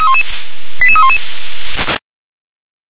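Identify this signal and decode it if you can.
Five-tone selective call (ZVEI), heard as quick runs of five tones, tones of about 70 ms. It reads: 88311
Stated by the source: ZVEI five-tone call received by SDR.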